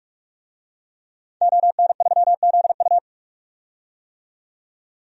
Morse code, heard as a keyed tone, about 46 wpm, 700 Hz. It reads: ON3ZU
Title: DXCC-100 Call Signs - 5 Characters - 45wpm